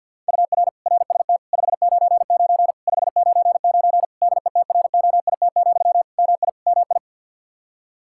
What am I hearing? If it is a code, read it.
Morse code, heard as a keyed tone, about 50 wpm, 700 Hz. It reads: UR RST 599 599 DETROIT, MI MI
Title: Practice QSO - Exchange 1 - Beginning - 50wpm